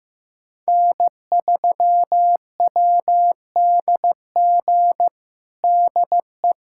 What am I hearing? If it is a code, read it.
Morse code, heard as a keyed tone, about 15 wpm, 700 Hz. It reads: N3WDG DE
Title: USA Callsign DE Callsign - 5 Characters - 15wpm